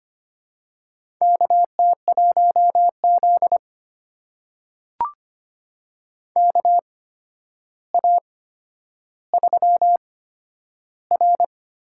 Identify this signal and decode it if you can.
Morse code, heard as a keyed tone, about 25 wpm, spaced as for 11 wpm, 700 Hz. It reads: KT1Z KA3R
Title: USA Call Signs - 4 Characters - 8x Character Spacing - 25wpm